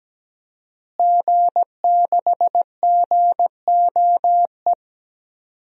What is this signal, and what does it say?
Morse code, heard as a keyed tone, about 17 wpm, 700 Hz. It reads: G6GOE
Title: DXCC-100 Call Signs - 5 Characters - Repeated 2x - 17wpm